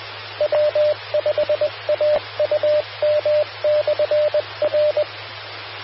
Morse code, heard as a keyed tone, about 20 wpm, 600 Hz. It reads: W5AUM/R
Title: Austin, TX Amateur Repeaters